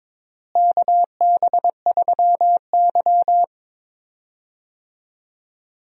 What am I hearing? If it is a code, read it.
Morse code, heard as a keyed tone, about 22 wpm, 700 Hz. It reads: KB3Y
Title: USA Call Signs - 4 Characters - Repeated 3x - 22wpm